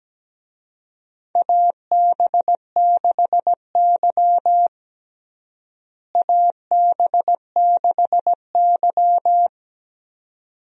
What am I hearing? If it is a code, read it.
Morse code, heard as a keyed tone, about 17 wpm, 700 Hz. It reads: AB6Y AB6Y
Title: USA Call Signs - 4 Characters - Repeated 2x - 17wpm